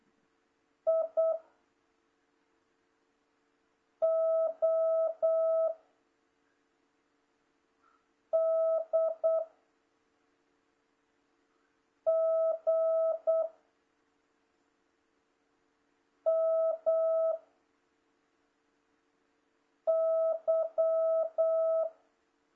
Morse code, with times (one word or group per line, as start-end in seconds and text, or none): 0.9-1.3 I
4.0-5.7 O
8.3-9.4 D
12.1-13.4 G
16.3-17.3 M
19.9-21.8 Y